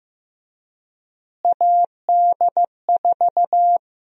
Morse code, heard as a keyed tone, about 15 wpm, 700 Hz. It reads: AD4
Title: USA Call Sign Prefixes - Repeated 2x - 15wpm